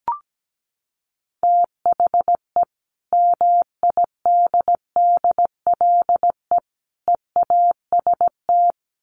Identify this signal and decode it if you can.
Morse code, heard as a keyed tone, about 17 wpm, 700 Hz. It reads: THE MIDDLE EAST
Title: Sets of 3 Words 17wpm